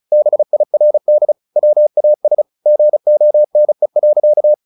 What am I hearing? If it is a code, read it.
Morse code, heard as a keyed tone, about 34 wpm, 600 Hz. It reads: BIRD WAS GONE.
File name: the old man and the sea / the old man and the sea 1 35wpm